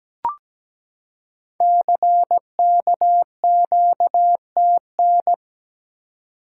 Morse code, heard as a keyed tone, about 17 wpm, 700 Hz. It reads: CKQTN
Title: Code Groups - 5 Letters - 17wpm